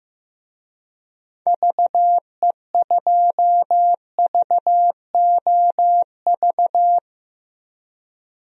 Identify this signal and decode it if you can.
Morse code, heard as a keyed tone, about 15 wpm, 700 Hz. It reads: VE2VOV
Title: DXCC-100 Call Signs - 6 Characters - Repeated 3x - 15wpm